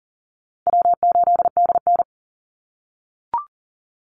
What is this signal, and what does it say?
Morse code, heard as a keyed tone, about 40 wpm, 700 Hz. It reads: W8DN
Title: Call Signs - 40wpm